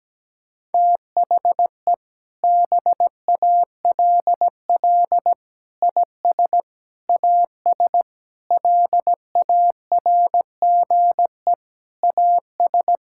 Morse code, heard as a keyed tone, about 17 wpm, 700 Hz. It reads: THE BALL IS AS LARGE AS
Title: Sentences from Top 1000 Words - Review Part 6 - 17wpm